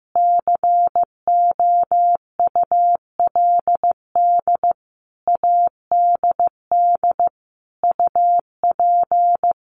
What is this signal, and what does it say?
Morse code, heard as a keyed tone, about 15 wpm, 700 Hz. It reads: COULD ADD UP
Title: A Sound Of Thunder 15wpm